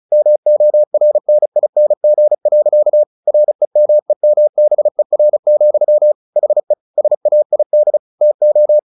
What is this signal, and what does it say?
Morse code, heard as a keyed tone, about 34 wpm, 600 Hz. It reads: MORNING. REMEMBER, HE SAID TO
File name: the old man and the sea / the old man and the sea 1 35wpm